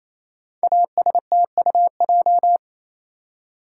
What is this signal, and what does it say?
Morse code, heard as a keyed tone, about 28 wpm, 700 Hz. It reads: ASTUJ
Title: Code Groups - 5 Letters - 28wpm